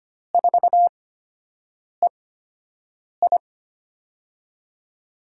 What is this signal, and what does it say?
Morse code, heard as a keyed tone, about 25 wpm, spaced as for 7 wpm, 700 Hz. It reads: 4EI
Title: USA Call Signs - 5 Characters - 8x Character Spacing - 25wpm